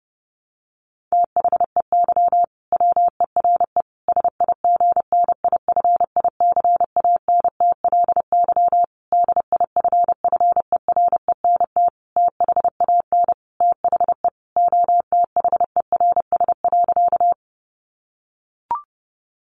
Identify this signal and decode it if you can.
Morse code, heard as a keyed tone, about 30 wpm, 700 Hz. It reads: THEY WERE SIGNIFICANTLY DIFFERENT THAN THE OTHERS.
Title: Sentences from Top 1500 Words 30wpm